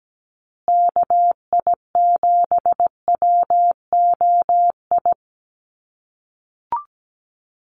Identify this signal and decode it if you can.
Morse code, heard as a keyed tone, about 17 wpm, 700 Hz. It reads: KI7WOI
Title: USA Call Signs Repeated 2x 17wpm